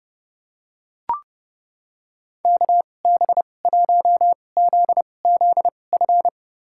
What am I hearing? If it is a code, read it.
Morse code, heard as a keyed tone, about 30 wpm, spaced as for 22 wpm, 700 Hz. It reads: KB1ZZF
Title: USA Call Signs - 2x Character Spacing - 30wpm